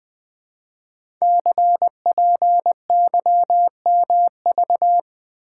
Morse code, heard as a keyed tone, about 20 wpm, 700 Hz. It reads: CPYMV
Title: Code Groups - 5 Letter:Numbers - 20wpm